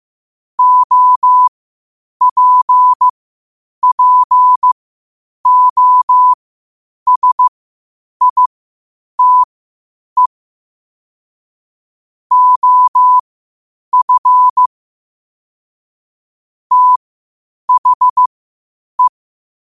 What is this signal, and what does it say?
Morse code, heard as a keyed tone, about 15 wpm, spaced as for 8 wpm, 1000 Hz. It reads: OPPOSITE OF THE
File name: 8wpm